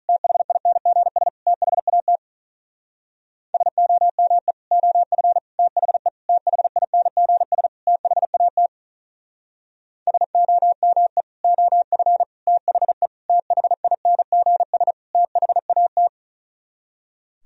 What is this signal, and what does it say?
Morse code, heard as a keyed tone, about 44 wpm, 700 Hz. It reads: THINGS THAT SOME OF THE THINGS THAT SOME OF THE THINGS THAT
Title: Speed-Racing - Sets of 5 Words 35wpm